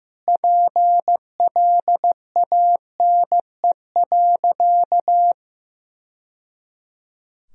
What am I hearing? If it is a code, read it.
Morse code, heard as a keyed tone, about 15 wpm, 700 Hz. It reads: PLANE.
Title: Sentences from Top 1000 Words - Review Part 3 - 15wpm